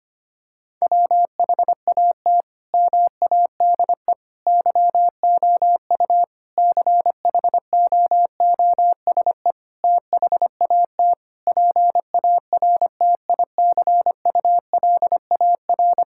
Morse code, heard as a keyed tone, about 25 wpm, 700 Hz. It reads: WHAT MADE YOU CHOOSE THAT PARTICULAR
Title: Sentences from Top 1000 Words - Review Part 5 - 25wpm